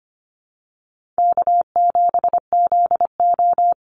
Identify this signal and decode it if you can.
Morse code, heard as a keyed tone, about 25 wpm, 700 Hz. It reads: K7ZO
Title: NA SOTA Chaser Callsigns 25wpm